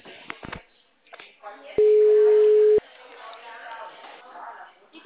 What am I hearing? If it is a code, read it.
Ringback tone (Europe, 425 Hz), with speech.
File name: out- -2014